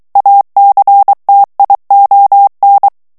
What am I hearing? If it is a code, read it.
Morse code, heard as a keyed tone, about 23 wpm, 800 Hz. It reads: ACTION
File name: ANGLAIS texte clair 3 vitesse 22 mots minute